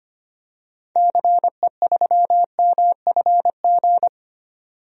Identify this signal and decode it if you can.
Morse code, heard as a keyed tone, about 25 wpm, 700 Hz. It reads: CE3MFG
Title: DXCC-100 Call Signs - 6 Characters - 25wpm